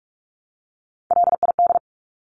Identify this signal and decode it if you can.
Morse code, heard as a keyed tone, about 46 wpm, 700 Hz. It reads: LID
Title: Abbreviations 45wpm